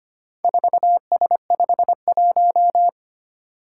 Morse code, heard as a keyed tone, about 25 wpm, 700 Hz. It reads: 4S51
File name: practice / 25wpm / Four Letter-Number Combinations - 25wpm